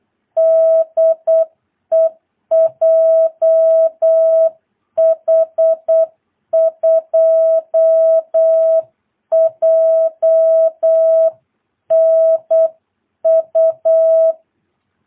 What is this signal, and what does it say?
Morse code, heard as a keyed tone, about 8 wpm, 660 Hz. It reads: DE JH2JNU
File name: jh8cbh / 2022cw